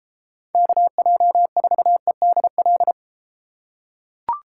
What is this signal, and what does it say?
Morse code, heard as a keyed tone, about 32 wpm, 700 Hz. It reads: KJ4EDL